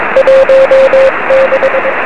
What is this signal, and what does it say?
Morse code, heard as a keyed tone, about 22 wpm, 530 Hz, with radio noise. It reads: 16
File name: 1823z yl2aj 23cm